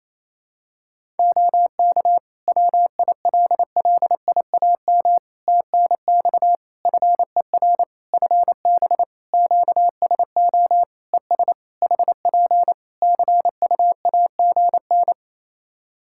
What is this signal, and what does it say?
Morse code, heard as a keyed tone, about 28 wpm, 700 Hz. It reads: OK WILLIAM TNX FER FB QSO ES HP CUAGN
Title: Practice QSO - Exchange 4 - Ending - 28wpm